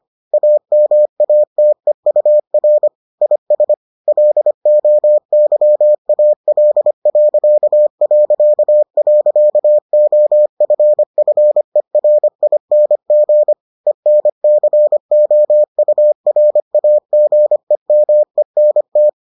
Morse code, heard as a keyed tone, about 25 wpm, 600 Hz. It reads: AMATEUR IS LOYAL...OFFERING ENCOURAGEMENT